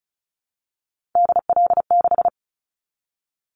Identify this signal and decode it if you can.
Morse code, heard as a keyed tone, about 34 wpm, 700 Hz. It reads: DL6
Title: Callsign Prefixes for Top 25 DXCC - 35wpm